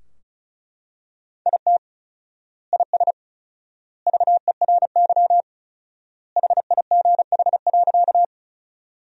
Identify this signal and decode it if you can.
Morse code, heard as a keyed tone, about 34 wpm, 700 Hz. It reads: IT IS VERY HIGH.
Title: 4x Word-Spacing - Sentences from Top 200 Words - 35wpm